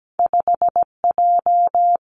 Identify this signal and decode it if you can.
Morse code, heard as a keyed tone, about 17 wpm, 700 Hz. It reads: 5 J
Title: Two Challenging Letter-Number Combinations 17wpm